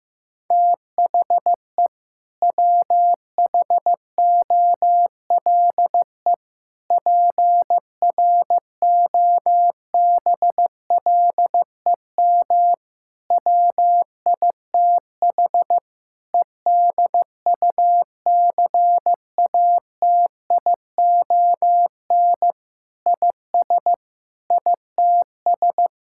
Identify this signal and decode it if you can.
Morse code, heard as a keyed tone, about 15 wpm, 700 Hz. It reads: THE WHOLE PROBLEM WITH EDUCATION IS ITS
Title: Sentences from Top 1000 Words - Review Part 6 - 15wpm